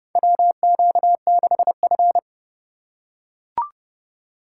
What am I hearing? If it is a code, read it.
Morse code, heard as a keyed tone, about 30 wpm, 700 Hz. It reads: WQ6F
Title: USA Call Signs - 4 Characters - 3x Character Spacing - 30wpm